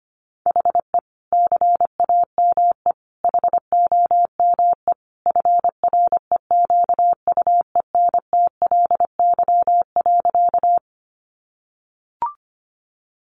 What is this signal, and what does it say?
Morse code, heard as a keyed tone, about 25 wpm, 700 Hz. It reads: HE CAME HOME FREQUENTLY.
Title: Sentences from Top 1300 Words 25wpm